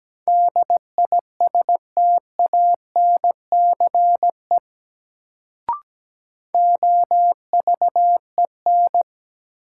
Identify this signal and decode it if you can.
Morse code, heard as a keyed tone, about 17 wpm, 700 Hz. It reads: DISTANCE OVEN
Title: Sentences from Top 1200 Words - New Word Drill Alt - 17wpm